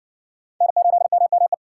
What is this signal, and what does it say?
Morse code, heard as a keyed tone, about 60 wpm, 700 Hz. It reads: N8MME